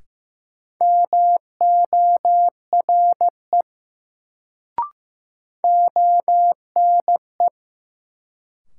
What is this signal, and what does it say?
Morse code, heard as a keyed tone, about 15 wpm, 700 Hz. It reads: MORE ONE
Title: Sentences from Top 100 Words - New Word Drill - 15wpm